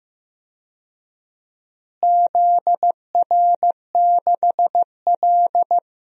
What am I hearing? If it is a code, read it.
Morse code, heard as a keyed tone, about 15 wpm, 700 Hz. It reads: ZR6L
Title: DXCC-100 Call Signs - 4 Characters - Repeated 2x - 15wpm